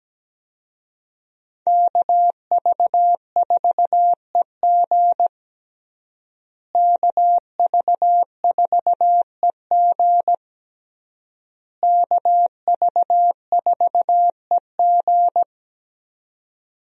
Morse code, heard as a keyed tone, about 17 wpm, 700 Hz. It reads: KV4EG KV4EG KV4EG
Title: USA Call Signs - 5 Characters - Repeated 3x - 17wpm